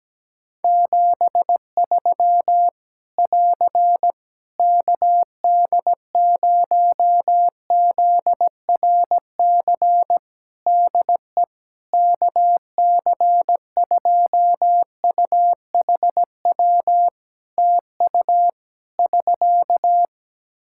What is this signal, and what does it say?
Morse code, heard as a keyed tone, about 17 wpm, 700 Hz. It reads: 73 <AR> KD0ZRC DE KC2UHW TU <SK>
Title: Practice QSO - Exchange 4 - Ending - 17wpm